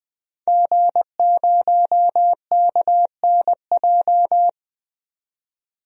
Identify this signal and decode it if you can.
Morse code, heard as a keyed tone, about 20 wpm, 700 Hz. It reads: G0KNJ
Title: DXCC-100 Call Signs - 5 Characters - 20wpm